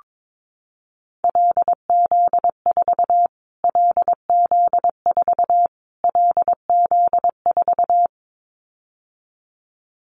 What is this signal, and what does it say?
Morse code, heard as a keyed tone, about 22 wpm, 700 Hz. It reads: LZ4 LZ4 LZ4
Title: Callsign Prefixes for Top 25 DXCC - Repeated 3x - 22wpm